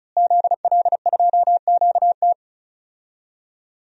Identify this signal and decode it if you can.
Morse code, heard as a keyed tone, about 36 wpm, 700 Hz. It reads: ZL2QT
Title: DXCC-100 Call Signs - 5 Characters - Repeated 3x - 35wpm